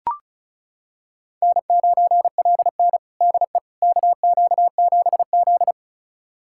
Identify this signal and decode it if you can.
Morse code, heard as a keyed tone, about 34 wpm, 700 Hz. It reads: N9LN DE KQ7Z
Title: USA Callsign DE Callsign - 4 Characters - 35wpm